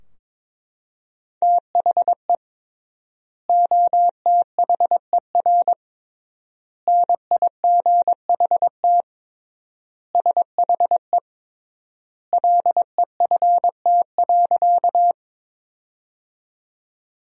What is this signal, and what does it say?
Morse code, heard as a keyed tone, about 22 wpm, 700 Hz. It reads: THE OTHER NIGHT SHE LEFT.
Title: 3x Word-Spacing - Sentences from Top 300 Words - 22wpm